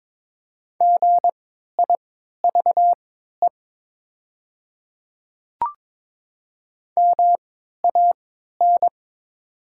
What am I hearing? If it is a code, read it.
Morse code, heard as a keyed tone, about 22 wpm, spaced as for 13 wpm, 700 Hz. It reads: GIVE MAN
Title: 3x Character Spacing - Top 200 Words - 22wpm